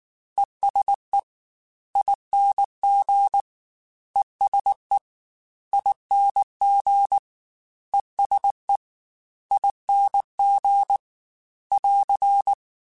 Morse code, heard as a keyed tone, about 19 wpm, 800 Hz. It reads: ESE ING ESE ING ESE ING <AR>